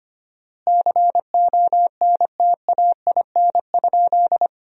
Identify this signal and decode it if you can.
Morse code, heard as a keyed tone, about 25 wpm, 700 Hz. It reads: CONTAIN?
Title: 4x Word-Spacing - Sentences from Top 500 Words - 25wpm